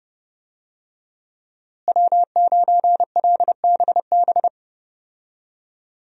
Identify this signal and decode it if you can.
Morse code, heard as a keyed tone, about 30 wpm, 700 Hz. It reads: W9LBB
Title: USA Call Signs - 5 Characters - Repeated 3x - 30wpm